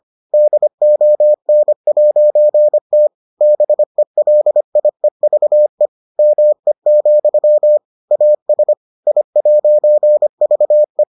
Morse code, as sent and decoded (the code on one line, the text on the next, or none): -.. --- -. .----. - / -... .-.. .. ...- . / -- . --..-- / .- ... / .. .----. ...- .
DON'T BELIEVE ME, AS I'VE